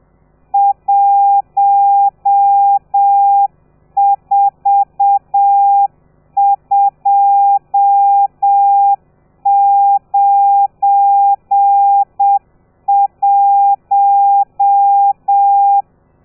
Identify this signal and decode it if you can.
Morse code, heard as a keyed tone, about 7 wpm, 800 Hz. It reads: 14291
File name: telegrafia / i2tfj / Lez12 30